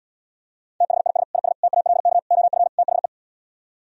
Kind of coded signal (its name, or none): Morse code